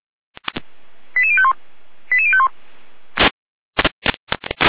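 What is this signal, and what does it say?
Five-tone selective call (ZVEI), heard as quick runs of five tones, tones of about 70 ms. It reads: 88051 88051
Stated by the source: ZVEI five-tone selective call received on an SDR.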